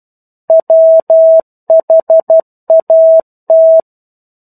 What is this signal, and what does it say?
Morse code, heard as a keyed tone, about 12 wpm, 650 Hz. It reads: WHAT
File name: 01-Chapter-01 12 WPM